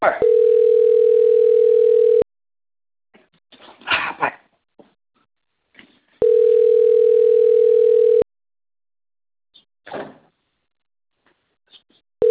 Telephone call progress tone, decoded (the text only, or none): ringback tone